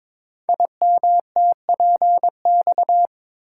Morse code, heard as a keyed tone, about 22 wpm, 700 Hz. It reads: IMTPX
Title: Code Groups - 5 Letters - 22wpm